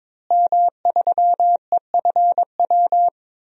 Morse code, heard as a keyed tone, about 22 wpm, 700 Hz. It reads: M3EFW
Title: DXCC-100 Call Signs - 5 Characters - Repeated 2x - 22wpm